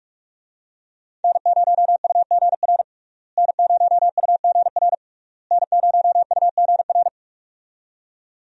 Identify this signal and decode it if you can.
Morse code, heard as a keyed tone, about 46 wpm, 700 Hz. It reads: N0UGR N0UGR N0UGR
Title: USA Call Signs - 5 Characters - Repeated 3x - 45wpm